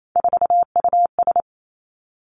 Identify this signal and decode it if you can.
Morse code, heard as a keyed tone, about 28 wpm, 700 Hz. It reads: 4US
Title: Three Challenging Letter-Number Combinations 28wpm